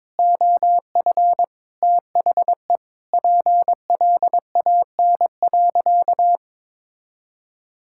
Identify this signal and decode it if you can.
Morse code, heard as a keyed tone, about 21 wpm, 700 Hz. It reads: OF THE PLAN.
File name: practice / 22wpm / Sentences with 8-Word Phrases - 22wpm